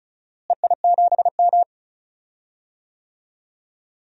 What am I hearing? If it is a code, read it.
Morse code, heard as a keyed tone, about 34 wpm, 700 Hz. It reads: EI7M
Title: DXCC-100 Call Signs - 4 Characters - Repeated 3x - 35wpm